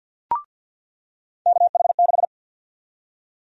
Morse code, heard as a keyed tone, about 47 wpm, 700 Hz. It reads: KH6
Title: USA Call Sign Prefixes - 50wpm